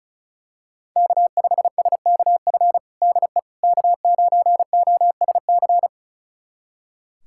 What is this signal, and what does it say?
Morse code, heard as a keyed tone, about 36 wpm, 700 Hz. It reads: K5SKF DE K9OSC
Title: USA Callsign DE Callsign - 5 Characters - 35wpm